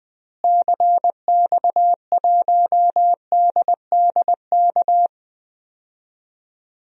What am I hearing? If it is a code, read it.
Morse code, heard as a keyed tone, about 21 wpm, 700 Hz. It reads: CX1DDK
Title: DXCC-100 Call Signs - 6 Characters - Repeated 3x - 20wpm